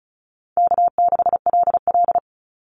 Morse code, heard as a keyed tone, about 34 wpm, 700 Hz. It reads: K6LL
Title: Easy Call Signs Repeated 2x - 35wpm